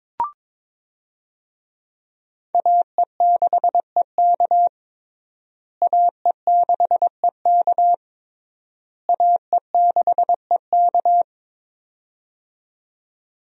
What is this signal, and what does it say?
Morse code, heard as a keyed tone, about 21 wpm, 700 Hz. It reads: AE6EK AE6EK AE6EK
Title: USA Call Signs - 5 Characters - Repeated 3x - 22wpm